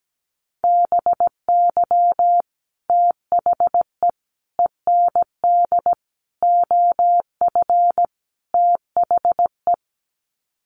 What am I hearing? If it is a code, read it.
Morse code, heard as a keyed tone, about 17 wpm, 700 Hz. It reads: BY THE END OF THE
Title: Sets of 5 Words 17wpm